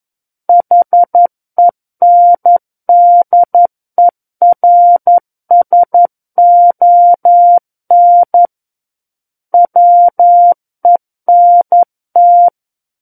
Morse code, heard as a keyed tone, about 11 wpm, 700 Hz. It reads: HENDERSON WENT
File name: Book One / 02-Chapter 2 - 11 WPM